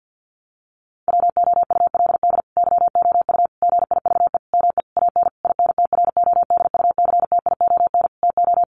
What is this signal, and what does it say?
Morse code, heard as a keyed tone, about 50 wpm, 700 Hz. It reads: WOULD YOU GIVE ME AN INTRODUCTION TO
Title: Sentences from Top 1000 Words 50wpm